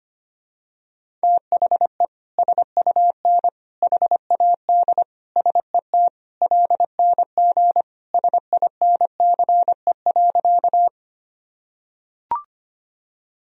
Morse code, heard as a keyed tone, about 25 wpm, 700 Hz. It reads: THE SUN HAD SET LNG SINCE.
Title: Sentences from Top 300 Words and Abbreviations 25wpm